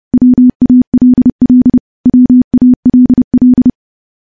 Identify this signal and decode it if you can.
Morse code, heard as a keyed tone, about 30 wpm, 250 Hz. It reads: WALL WALL
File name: morse / morse-training-30WPM